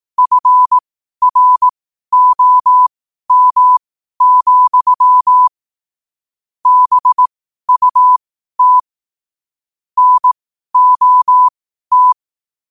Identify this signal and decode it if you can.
Morse code, heard as a keyed tone, about 18 wpm, spaced as for 12 wpm, 1000 Hz. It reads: FROM, BUT NOT
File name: 12wpm